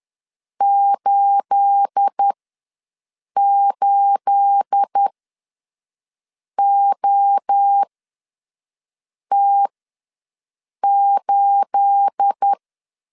Morse code, heard as a keyed tone, about 10 wpm, spaced as for 5 wpm, 800 Hz. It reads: 88OT8
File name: LEZIONI / 9 28 ws 10wpm 4124